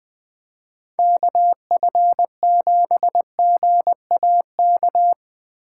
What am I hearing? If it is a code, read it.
Morse code, heard as a keyed tone, about 21 wpm, 700 Hz. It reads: KF7GAK
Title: Speed Racing - USA 2x3 Call Signs 20wpm